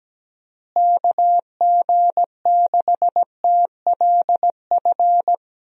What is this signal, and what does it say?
Morse code, heard as a keyed tone, about 17 wpm, 700 Hz. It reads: KG6TLF
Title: USA Call Signs - 6 Characters - Repeated 3x - 17wpm